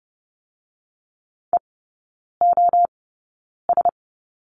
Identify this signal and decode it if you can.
Morse code, heard as a keyed tone, about 30 wpm, spaced as for 9 wpm, 700 Hz. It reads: EOS
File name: Lesson 18e - Letter m - Three Characters 30wpm